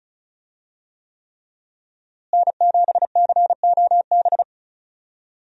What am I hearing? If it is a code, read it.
Morse code, heard as a keyed tone, about 36 wpm, 700 Hz. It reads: N7COB
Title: USA Call Signs - 5 Characters - Repeated 3x - 35wpm